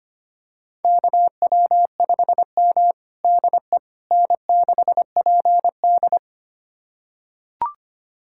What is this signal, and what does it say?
Morse code, heard as a keyed tone, about 25 wpm, 700 Hz. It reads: KW5M DE N6PD
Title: USA Callsign DE Callsign - 4 Characters - 25wpm